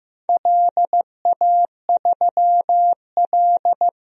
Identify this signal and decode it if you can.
Morse code, heard as a keyed tone, about 15 wpm, 700 Hz. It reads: LA3L
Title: DXCC-100 Call Signs - 4 Characters - 15wpm